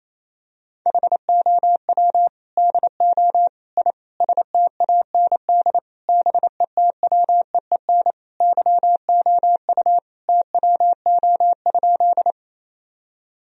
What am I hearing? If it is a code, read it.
Morse code, heard as a keyed tone, about 28 wpm, 700 Hz. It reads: HOW DO I STAND BETWEEN YOU TWO?
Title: Sentences from Top 1000 Words - Review Part 1 - 28wpm